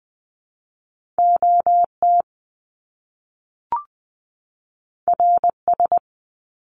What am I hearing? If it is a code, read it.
Morse code, heard as a keyed tone, about 20 wpm, 700 Hz. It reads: OT RS
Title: Top Two Letter Combinations in Words 20wpm